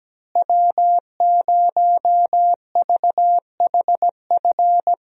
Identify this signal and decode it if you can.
Morse code, heard as a keyed tone, about 17 wpm, 700 Hz. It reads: W0VHF
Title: USA Call Signs - 5 Characters - Repeated 3x - 17wpm